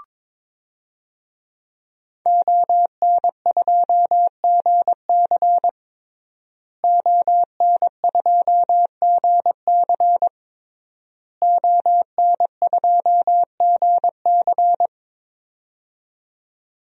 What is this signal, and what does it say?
Morse code, heard as a keyed tone, about 22 wpm, 700 Hz. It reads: ON2GC ON2GC ON2GC
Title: DXCC-100 Call Signs - 5 Characters - Repeated 3x - 22wpm